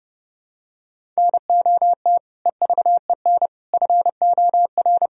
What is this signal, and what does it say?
Morse code, heard as a keyed tone, about 30 wpm, 700 Hz. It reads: NOT EVEN FOR
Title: Sentences from Top 1000 Words - Review Part 1 - 30wpm